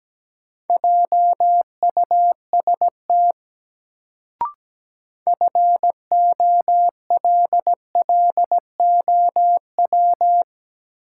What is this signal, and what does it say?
Morse code, heard as a keyed tone, about 17 wpm, 700 Hz. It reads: JUST FOLLOW
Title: Sentences from Top 200 Words - New Word Drill - 17wpm